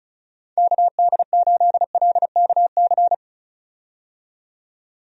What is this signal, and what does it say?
Morse code, heard as a keyed tone, about 36 wpm, 700 Hz. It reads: KD8LKC
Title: USA Call Signs - 6 Characters - 35wpm